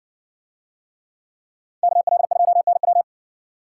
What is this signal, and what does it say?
Morse code, heard as a keyed tone, about 60 wpm, 700 Hz. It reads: KL2NW